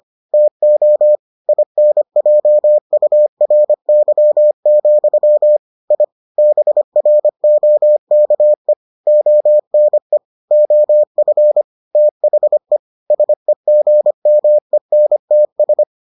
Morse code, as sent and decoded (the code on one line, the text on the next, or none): - --- / .. -. .--- ..- .-. -.-- --..-- / .. / -... .-. --- -.- . / --- -. / --- ..-. / - .... / ... --. -- . -. - ...
TO INJURY, I BROKE ONE OF THE SEGMENTS